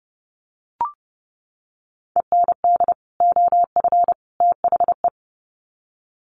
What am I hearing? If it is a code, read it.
Morse code, heard as a keyed tone, about 30 wpm, 700 Hz. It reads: END OF THE
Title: Sets of 3 Words 30wpm